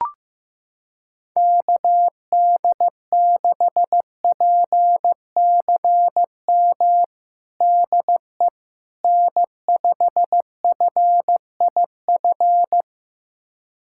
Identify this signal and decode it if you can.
Morse code, heard as a keyed tone, about 15 wpm, 700 Hz. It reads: KD6PCM DE N5FIF
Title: USA Callsign DE Callsign 15wpm